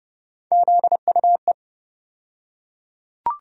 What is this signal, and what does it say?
Morse code, heard as a keyed tone, about 30 wpm, 700 Hz. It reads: ZUE
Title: USA Call Sign Postfixes - 3 Letters 30wpm